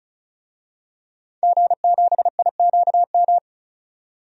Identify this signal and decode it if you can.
Morse code, heard as a keyed tone, about 34 wpm, 700 Hz. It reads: G7IQM
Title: DXCC-100 Call Signs - 5 Characters - 35wpm